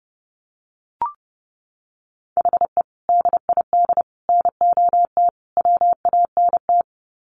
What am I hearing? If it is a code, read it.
Morse code, heard as a keyed tone, about 30 wpm, 700 Hz. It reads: HE DID NOT WANT
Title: Sets of 4 Words 30wpm